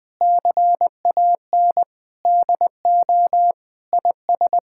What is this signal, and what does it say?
Morse code, heard as a keyed tone, about 20 wpm, 700 Hz. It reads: CAN DO IS
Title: Sentences with 7-Word Phrases - 20wpm